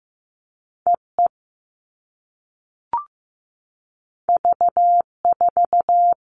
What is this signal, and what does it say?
Morse code, heard as a keyed tone, about 15 wpm, 700 Hz. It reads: EE V4
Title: All Cut Numbers - 2-Digit 15wpm